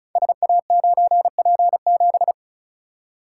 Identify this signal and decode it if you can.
Morse code, heard as a keyed tone, about 36 wpm, 690 Hz. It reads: SA9P7
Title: Code Groups - 5 Letter:Numbers - 35wpm